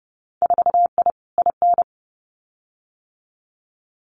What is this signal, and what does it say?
Morse code, heard as a keyed tone, about 30 wpm, 700 Hz. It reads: 4I IN
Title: Winter Field Day Exchanges Encore 30wpm